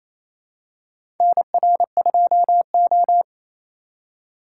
Morse code, heard as a keyed tone, about 28 wpm, 700 Hz. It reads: NR2O
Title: USA Call Signs - 4 Characters - 3x Character Spacing - 28wpm